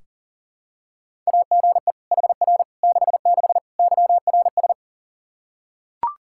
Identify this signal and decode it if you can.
Morse code, heard as a keyed tone, about 40 wpm, 700 Hz. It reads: AGE HR 66 YRS
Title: Practice QSO - Exchange 3 - More Info - 40wpm